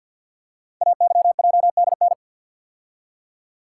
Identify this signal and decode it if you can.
Morse code, heard as a keyed tone, about 48 wpm, 690 Hz. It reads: AYJDN
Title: Code Groups - 5 Letters - 50wpm